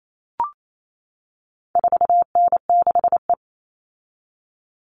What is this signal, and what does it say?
Morse code, heard as a keyed tone, about 28 wpm, 700 Hz. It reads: 4N6E
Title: All Cut Numbers - 4-Digit 28wpm